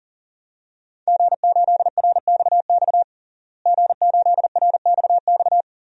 Morse code, heard as a keyed tone, about 40 wpm, 700 Hz. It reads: G8RXX G8RXX
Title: DXCC-100 Call Signs - 5 Characters - Repeated 2x - 40wpm